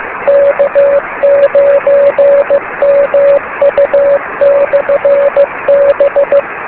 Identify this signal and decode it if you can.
Morse code, heard as a keyed tone, about 15 wpm, 560 Hz, with noise. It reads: K9MU/B
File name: beacons / k9mu